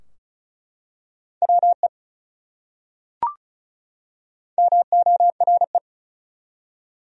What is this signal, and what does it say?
Morse code, heard as a keyed tone, about 34 wpm, 700 Hz. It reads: WE MORE
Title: Top 100 Words 35wpm